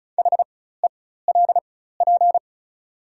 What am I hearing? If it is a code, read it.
Morse code, heard as a keyed tone, about 36 wpm, spaced as for 16 wpm, 700 Hz. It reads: HELP
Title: 4x Character Spacing - Top 300 Words - 35wpm